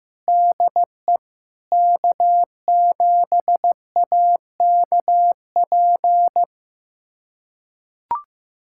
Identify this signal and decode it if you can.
Morse code, heard as a keyed tone, about 15 wpm, 700 Hz. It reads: DE K7AKP
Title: USA Callsign DE Callsign - 5 Characters - 15wpm